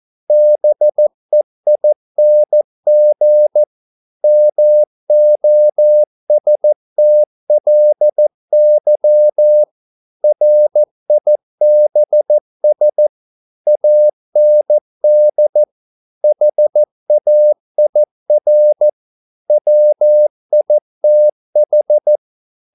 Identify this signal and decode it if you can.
Morse code, heard as a keyed tone, about 14 wpm, 600 Hz. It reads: BEING MOSTLY RIBS AND HAIR WITH
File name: mp3 files / a dogs mistake - 14wpm